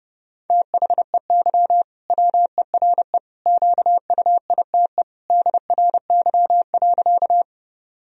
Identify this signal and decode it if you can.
Morse code, heard as a keyed tone, about 30 wpm, 700 Hz. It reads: THEY WERE QUITE DRY.
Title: Sentences from Top 1000 Words - Review Part 3 - 30wpm